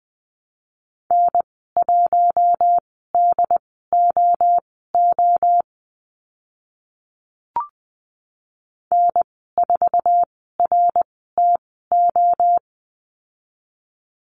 Morse code, heard as a keyed tone, about 20 wpm, spaced as for 15 wpm, 700 Hz. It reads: N1DOO N4RTO
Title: 088 Lesson 15g - Number 5 - Call Signs 20wpm